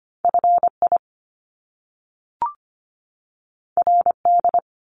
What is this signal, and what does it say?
Morse code, heard as a keyed tone, about 25 wpm, 700 Hz. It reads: F I R D
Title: Top Two Letter Combinations in Words 25wpm